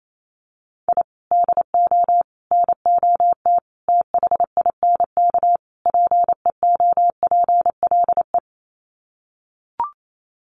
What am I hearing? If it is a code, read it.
Morse code, heard as a keyed tone, about 28 wpm, 700 Hz. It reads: I DO NOT THINK PEOPLE
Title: Sets of 5 Words Encore2 28wpm